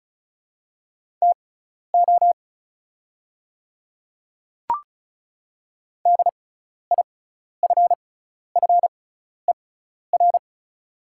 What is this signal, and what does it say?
Morse code, heard as a keyed tone, about 34 wpm, spaced as for 12 wpm, 700 Hz. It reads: TO DIFFER
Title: 6x Character Spacing - Top 200 Words - 35wpm